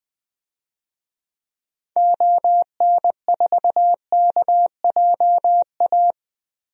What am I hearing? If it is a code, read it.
Morse code, heard as a keyed tone, about 20 wpm, 700 Hz. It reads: ON4KJA
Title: DXCC-100 Call Signs - 6 Characters - Repeated 2x - 20wpm